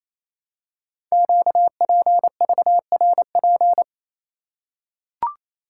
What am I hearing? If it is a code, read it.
Morse code, heard as a keyed tone, about 28 wpm, 700 Hz. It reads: QPVRP
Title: Code Groups - 5 Letters - 28wpm